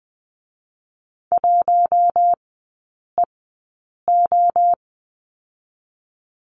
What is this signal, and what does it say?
Morse code, heard as a keyed tone, about 20 wpm, 700 Hz. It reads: 1 E O
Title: 107 Lesson 18e - Letter m - Three Characters 20wpm